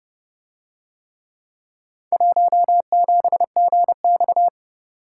Morse code, heard as a keyed tone, about 30 wpm, 700 Hz. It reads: 17GX